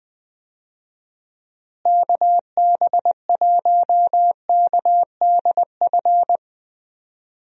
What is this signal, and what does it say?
Morse code, heard as a keyed tone, about 20 wpm, 700 Hz. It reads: KB1KDF
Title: USA Call Signs - 6 Characters - Repeated 3x - 20wpm